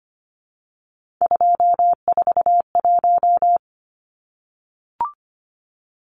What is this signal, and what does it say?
Morse code, heard as a keyed tone, about 25 wpm, 700 Hz. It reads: 241
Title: Numbers - 3-Digit 25wpm